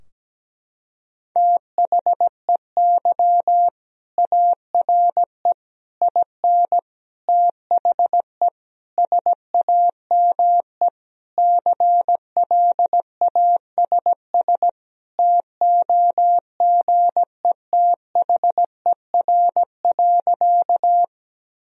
Morse code, heard as a keyed tone, about 17 wpm, 700 Hz. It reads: THEY ARE IN THE SAME CLASS TOGETHER.
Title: Sentences from Top 1000 Words - Review Part 2 - 17wpm